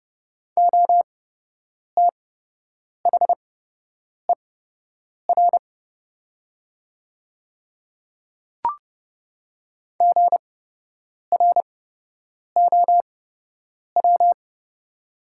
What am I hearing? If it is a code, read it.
Morse code, heard as a keyed tone, about 30 wpm, spaced as for 8 wpm, 700 Hz. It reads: OTHER GROW
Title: 8x Character Spacing - Top 300 Words - 30wpm